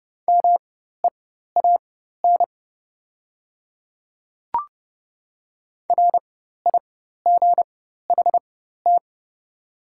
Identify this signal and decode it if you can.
Morse code, heard as a keyed tone, about 30 wpm, spaced as for 14 wpm, 700 Hz. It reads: MEAN RIGHT
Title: 4x Character Spacing - Top 200 Words - 30wpm